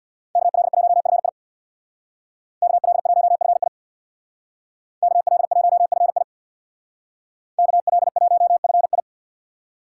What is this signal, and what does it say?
Morse code, heard as a keyed tone, about 64 wpm, 700 Hz. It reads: KL1FI KL1FI KL1FI KL1FI